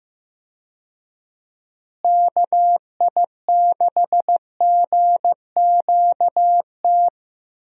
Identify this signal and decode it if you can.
Morse code, heard as a keyed tone, about 15 wpm, 700 Hz. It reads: KI6GQT
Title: USA Call Signs - 6 Characters - Repeated 2x - 15wpm